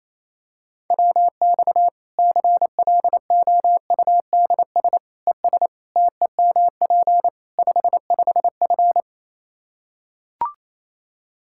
Morse code, heard as a keyed tone, about 28 wpm, 700 Hz. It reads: WX CLOUDS ES TEMP 55F
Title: Practice QSO - Exchange 2 - Info - 28wpm